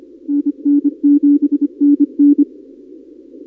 Morse code, heard as a keyed tone, about 26 wpm, 300 Hz. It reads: NN7NN
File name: contests / 2019-cw-wpx-kt5j-k5pi- -2330 0-31